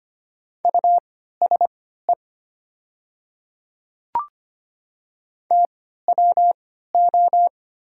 Morse code, heard as a keyed tone, about 25 wpm, spaced as for 14 wpm, 700 Hz. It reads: USE TWO
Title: 3x Character Spacing - Top 100 Words - 25wpm